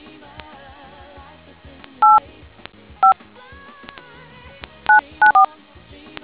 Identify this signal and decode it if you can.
DTMF, heard as telephone touch tones, tones of about 120 ms. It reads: 85 994